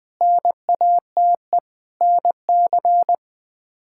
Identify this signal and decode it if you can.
Morse code, heard as a keyed tone, about 20 wpm, 700 Hz. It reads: NATE NC
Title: K1USN SST QSO Practice - 20wpm